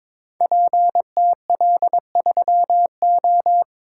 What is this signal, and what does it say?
Morse code, heard as a keyed tone, about 22 wpm, 700 Hz. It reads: PTL3O
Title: Code Groups - 5 Letter:Numbers - 22wpm